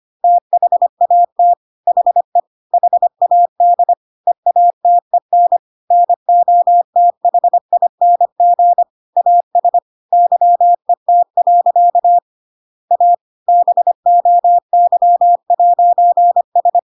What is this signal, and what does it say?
Morse code, heard as a keyed tone, about 25 wpm, 700 Hz. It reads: THAT HE HAD EATEN NOTHING AS YET. A BOY'S
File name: Pinocchio / Pinocchio ch5 25 25 ik1hge